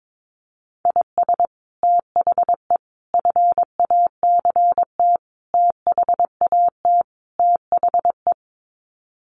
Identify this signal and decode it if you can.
Morse code, heard as a keyed tone, about 22 wpm, 700 Hz. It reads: IS THE FACT THAT THE
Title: Sets of 5 Words Encore2 22wpm